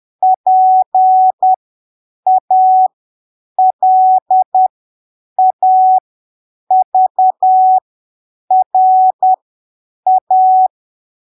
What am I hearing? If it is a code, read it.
Morse code, heard as a keyed tone, about 10 wpm, spaced as for 7 wpm, 750 Hz. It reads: PALAVRA
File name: arquivos / texto / texto 25 10ppm